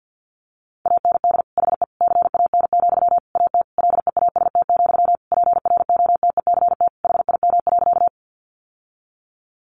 Morse code, heard as a keyed tone, about 50 wpm, 700 Hz. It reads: AND HE CAN, AT LEAST, PROTECT HIM.
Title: Sentences from Top 900 Words 50wpm